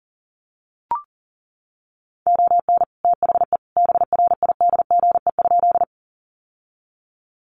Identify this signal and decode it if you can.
Morse code, heard as a keyed tone, about 40 wpm, 700 Hz. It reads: ON THE BRIDGE?
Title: Sentences from Top 1200 Words 40wpm